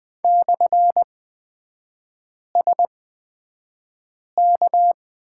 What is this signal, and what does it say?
Morse code, heard as a keyed tone, about 20 wpm, spaced as for 5 wpm, 700 Hz. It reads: /SK
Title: 224 Lesson 36e - BT - Three Characters 20wpm